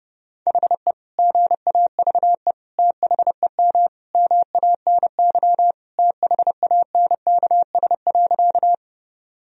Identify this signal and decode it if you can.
Morse code, heard as a keyed tone, about 30 wpm, 700 Hz. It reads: HE GAVE THEM MANY THANKS.
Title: Sentences from Top 1000 Words - Review Part 3 - 30wpm